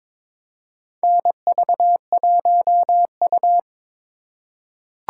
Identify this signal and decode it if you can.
Morse code, heard as a keyed tone, about 22 wpm, 700 Hz. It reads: NV1U
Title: K1USN SST QSO Practice - 22wpm